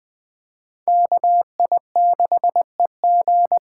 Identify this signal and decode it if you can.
Morse code, heard as a keyed tone, about 20 wpm, 700 Hz. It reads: KI6EG
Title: USA Call Signs - 5 Characters - 2x Character Spacing - 20wpm